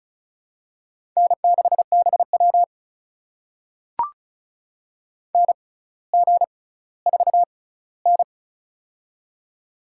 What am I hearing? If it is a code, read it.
Morse code, heard as a keyed tone, about 36 wpm, spaced as for 18 wpm, 700 Hz. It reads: N6BW NG4N
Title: USA Call Signs - 4 Characters - 6x Character Spacing - 35wpm